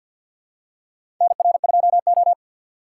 Morse code, heard as a keyed tone, about 53 wpm, 700 Hz. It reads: NR2O
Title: USA Call Signs - 4 Characters - 6x Character Spacing - 50wpm